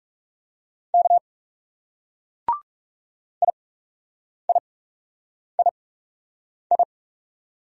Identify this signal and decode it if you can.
Morse code, heard as a keyed tone, about 38 wpm, spaced as for 8 wpm, 700 Hz. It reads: K IIII